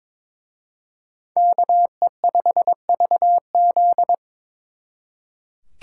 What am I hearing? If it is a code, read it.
Morse code, heard as a keyed tone, about 26 wpm, 700 Hz. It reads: KE5VZ
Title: Speed Racing - USA Call Signs 22wpm